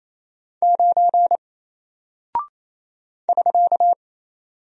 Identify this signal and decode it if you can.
Morse code, heard as a keyed tone, about 28 wpm, 700 Hz. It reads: 9 <SK>
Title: Single Character (Letter:Number:Punctuation:Prosign) Alt 28wpm